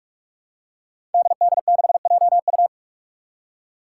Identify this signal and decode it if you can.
Morse code, heard as a keyed tone, about 46 wpm, 700 Hz. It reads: DD6JU
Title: Five Challenging Letter-Number Combinations 45wpm